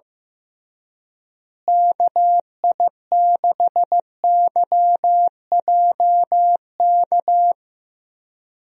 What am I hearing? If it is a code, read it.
Morse code, heard as a keyed tone, about 15 wpm, 700 Hz. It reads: KI6YJK
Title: USA Call Signs - 6 Characters - Repeated 3x - 15wpm